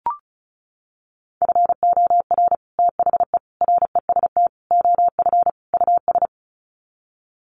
Morse code, heard as a keyed tone, about 34 wpm, 700 Hz. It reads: FOR THE REST OF US
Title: Sets of 5 Words Encore 35wpm